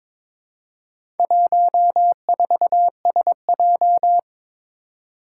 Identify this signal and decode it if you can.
Morse code, heard as a keyed tone, about 22 wpm, 700 Hz. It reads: 14SJ
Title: Four Challenging Letter-Number Combinations 22wpm